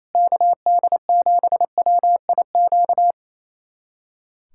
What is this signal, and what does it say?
Morse code, heard as a keyed tone, about 28 wpm, 700 Hz. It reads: KD7WIQ
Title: USA Callsign DE Callsign - 6 Characters - 28wpm